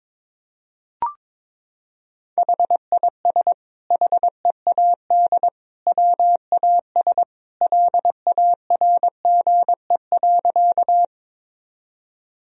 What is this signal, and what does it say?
Morse code, heard as a keyed tone, about 22 wpm, 700 Hz. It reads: HIS HEAD WAS LARGE.
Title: Sentences from Top 1000 Words - Review Part 1 - 22wpm